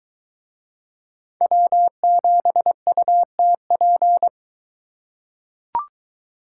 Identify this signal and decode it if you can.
Morse code, heard as a keyed tone, about 21 wpm, 700 Hz. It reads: W7UTP